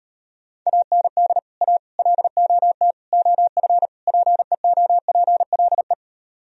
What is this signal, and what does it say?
Morse code, heard as a keyed tone, about 38 wpm, 700 Hz. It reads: AND A LOT OF PEOPLE